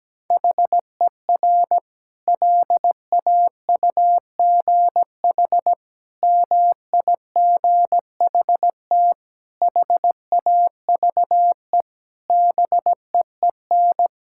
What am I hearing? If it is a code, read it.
Morse code, heard as a keyed tone, about 17 wpm, 700 Hz. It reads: HER LAUGH MIGHT HAVE BEEN
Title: Sentences from Top 1000 Words - Review Part 5 - 17wpm